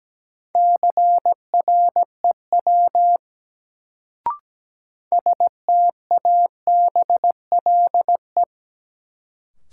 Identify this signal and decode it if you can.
Morse code, heard as a keyed tone, about 17 wpm, 700 Hz. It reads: CREW STABLE
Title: Sentences from Top 1500 Words - New Word Drill - 17wpm